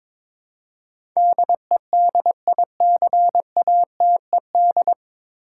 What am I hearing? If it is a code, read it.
Morse code, heard as a keyed tone, about 22 wpm, 700 Hz. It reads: DEDICATED
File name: 9 Letter Words - First 3 Letters Given - 22wpm